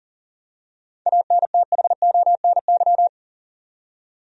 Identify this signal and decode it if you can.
Morse code, heard as a keyed tone, about 40 wpm, 700 Hz. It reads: ANTHONY
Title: USA Common Names - 40wpm